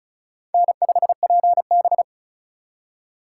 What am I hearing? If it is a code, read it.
Morse code, heard as a keyed tone, about 42 wpm, 700 Hz. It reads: N5PB
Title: Speed Racing - USA 1x2 Call Signs 35wpm